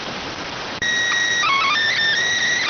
SSTV (slow-scan television), PD 290.